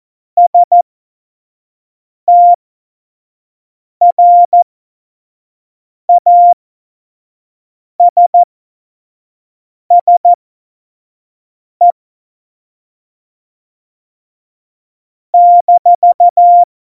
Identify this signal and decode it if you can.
Morse code, heard as a keyed tone, about 14 wpm, spaced as for 5 wpm, 700 Hz. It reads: STRASSE -
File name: Morsekurs / Witze / Witz 19. Juni 2018, 25 F70